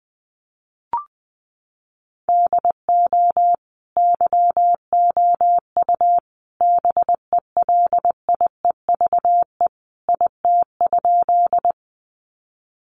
Morse code, heard as a keyed tone, about 20 wpm, 700 Hz. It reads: DO YOU BELIEVE IT?
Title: Sentences from Top 1000 Words 20wpm